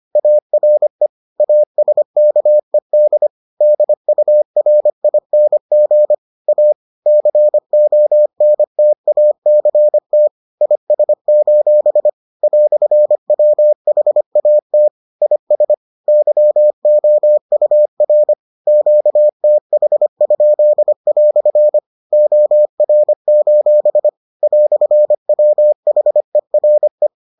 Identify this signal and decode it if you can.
Morse code, heard as a keyed tone, about 25 wpm, 600 Hz. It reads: ARE ASKED DURING A CONTACT IS: "WHAT IS YOUR QTH?" OR: "WHERE